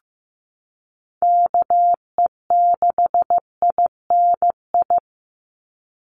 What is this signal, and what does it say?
Morse code, heard as a keyed tone, about 15 wpm, 700 Hz. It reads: KE6INI
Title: USA Call Signs 15wpm